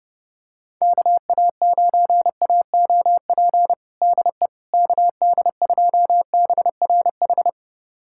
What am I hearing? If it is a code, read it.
Morse code, heard as a keyed tone, about 30 wpm, 700 Hz. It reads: KA9AOP DE KD2BRH
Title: USA Callsign DE Callsign - 6 Characters - 30wpm